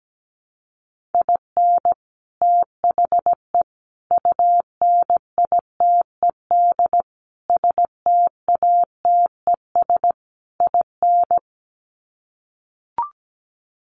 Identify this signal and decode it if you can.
Morse code, heard as a keyed tone, about 17 wpm, 700 Hz. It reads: IN THE UNITED STATES IN
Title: Sets of 5 Words 17wpm